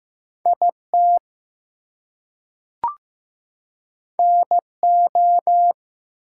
Morse code, heard as a keyed tone, about 15 wpm, 700 Hz. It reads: IT NO
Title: Two Letter Words - 15wpm